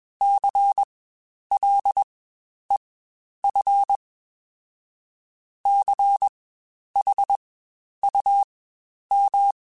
Morse code, heard as a keyed tone, about 21 wpm, spaced as for 10 wpm, 800 Hz. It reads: CLEF CHUM
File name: Spaced Alphas / Spaced Alphas mp3s / Cs sp @20